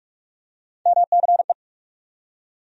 Morse code, heard as a keyed tone, about 42 wpm, 700 Hz. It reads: MCE
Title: USA Call Sign Postfixes - 3 Letters 45wpm